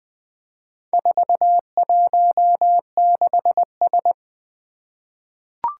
Morse code, heard as a keyed tone, about 20 wpm, 700 Hz. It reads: 416S
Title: Four Challenging Letter-Number Combinations 20wpm